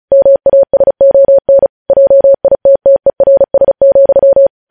Morse code, heard as a keyed tone, about 36 wpm, 550 Hz. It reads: MASON JITTERS,
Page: News Headlines in Morse Code at 35 WPM